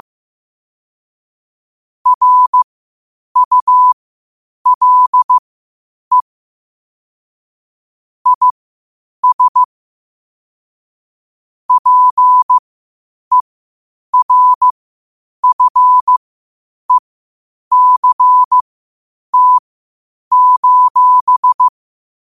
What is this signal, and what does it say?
Morse code, heard as a keyed tone, about 15 wpm, spaced as for 8 wpm, 1000 Hz. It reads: RULE IS PERFECT:
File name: qotdfiles / 8wpm